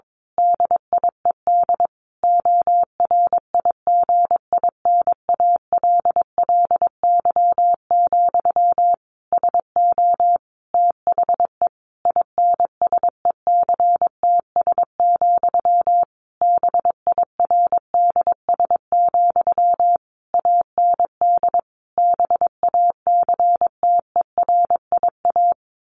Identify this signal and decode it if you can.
Morse code, heard as a keyed tone, about 22 wpm, 700 Hz. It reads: DIED ORIGINALLY, SO THE INSECTS, BIRDS, AND BACTERIA
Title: A Sound Of Thunder 22wpm